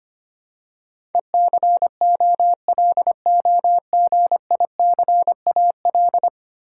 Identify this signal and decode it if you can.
Morse code, heard as a keyed tone, about 25 wpm, 700 Hz. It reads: ECOLOGICAL
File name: Ten Letter Words - 25wpm